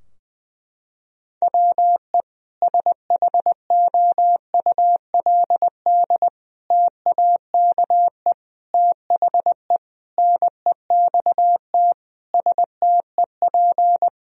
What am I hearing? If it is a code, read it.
Morse code, heard as a keyed tone, about 20 wpm, 700 Hz. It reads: WE SHOULD TAKE THE NEXT STEP
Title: Sentences from Top 1000 Words - Review Part 2 - 20wpm